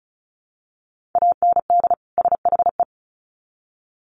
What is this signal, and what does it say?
Morse code, heard as a keyed tone, about 34 wpm, 700 Hz. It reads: AND SHE
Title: Sets of 2 Words 35wpm